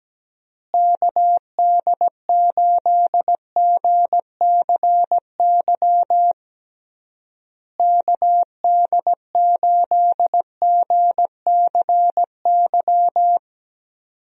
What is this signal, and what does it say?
Morse code, heard as a keyed tone, about 17 wpm, 700 Hz. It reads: KD8GCY KD8GCY
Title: USA Call Signs - 6 Characters - Repeated 3x - 17wpm